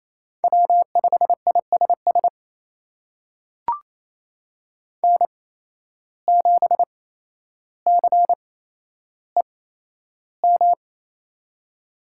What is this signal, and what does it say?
Morse code, heard as a keyed tone, about 28 wpm, spaced as for 12 wpm, 700 Hz. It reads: W5ISS N7CEM
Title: USA Call Signs - 5 Characters - 8x Character Spacing - 28wpm